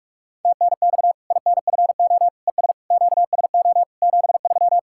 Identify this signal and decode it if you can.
Morse code, heard as a keyed tone, about 46 wpm, 700 Hz. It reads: TNX INFO ES QSO 73
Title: K1USN SST QSO Practice 45wpm